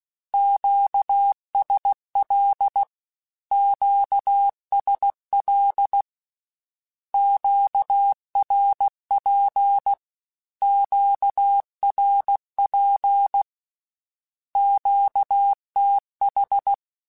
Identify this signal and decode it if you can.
Morse code, heard as a keyed tone, about 16 wpm, 800 Hz. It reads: QSL QSL QRP QRP QTH
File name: Q Calls / Q Calls-mp3s / Q Calls Twice @15